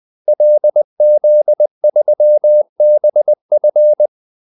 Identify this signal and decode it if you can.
Morse code, heard as a keyed tone, about 20 wpm, 600 Hz. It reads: LZ3BF
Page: Kaldesignaler 20 wpm